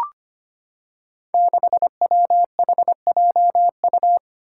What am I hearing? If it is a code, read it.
Morse code, heard as a keyed tone, about 25 wpm, 700 Hz. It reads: 6WHJU
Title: Five Challenging Letter-Number Combinations 25wpm